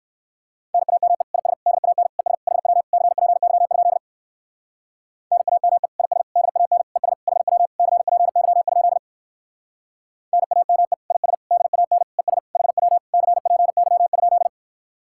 Morse code, heard as a keyed tone, about 58 wpm, 700 Hz. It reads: NAME IS DAN ES HW CPY? NAME IS DAN ES HW CPY? NAME IS DAN ES HW CPY?